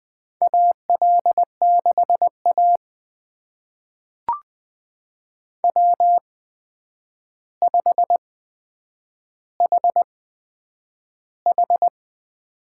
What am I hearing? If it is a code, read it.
Morse code, heard as a keyed tone, about 20 wpm, spaced as for 9 wpm, 700 Hz. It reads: AL6A W5HH
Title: USA Call Signs - 4 Characters - 8x Character Spacing - 20wpm